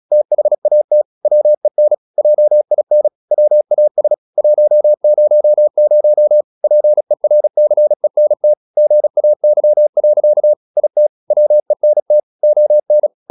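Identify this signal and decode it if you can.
Morse code, heard as a keyed tone, about 36 wpm, 600 Hz. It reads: THAT WEN JIN WAS 100 PERCENT GAY. IT WENT ON
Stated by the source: Morse practice at 36 wpm, text of Ken Follett's novel Never, from CW med Gnister.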